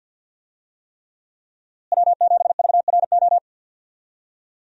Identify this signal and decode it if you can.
Morse code, heard as a keyed tone, about 51 wpm, 700 Hz. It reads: W7VRO
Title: USA Call Signs - 5 Characters - Repeated 3x - 50wpm